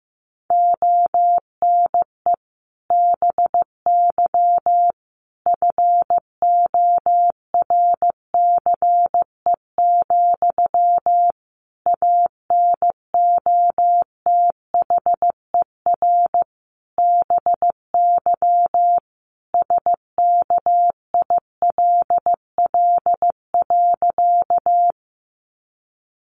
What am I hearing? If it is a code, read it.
Morse code, heard as a keyed tone, about 15 wpm, 700 Hz. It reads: ONE BY FORCE, ANOTHER BY SKILL.
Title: Sentences from Top 1200 Words 15wpm